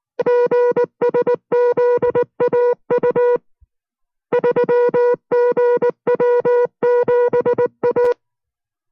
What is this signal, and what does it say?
Morse code, heard as a keyed tone, about 19 wpm, 490 Hz. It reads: GSZAU 3GW7A